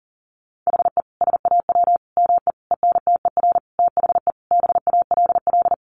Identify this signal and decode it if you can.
Morse code, heard as a keyed tone, about 37 wpm, 700 Hz. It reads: HE SAW ME ENTER THE BALL
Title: Sentences from Top 1300 Words 40wpm